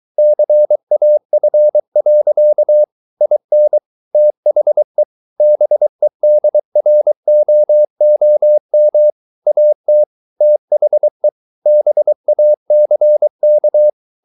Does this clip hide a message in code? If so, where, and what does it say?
Yes: Morse code from 0.2 to 13.9 s, heard as a keyed tone, about 23 wpm, 600 Hz. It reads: CAF. IN THE BEDROOM AT THE BACK